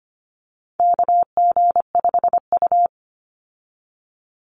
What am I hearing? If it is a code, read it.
Morse code, heard as a keyed tone, about 25 wpm, 700 Hz. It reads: KG5U
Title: Call Signs Repeated 3x - 25wpm